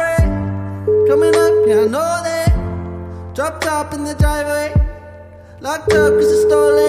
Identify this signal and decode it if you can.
Ringback tone (Europe, 425 Hz), with noise.